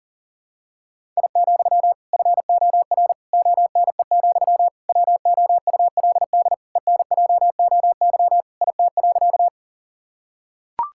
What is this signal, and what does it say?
Morse code, heard as a keyed tone, about 40 wpm, 700 Hz. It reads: I, FOR ONE, WOULD ENJOY IT.
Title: Sentences from Top 1000 Words - Review Part 5 - 40wpm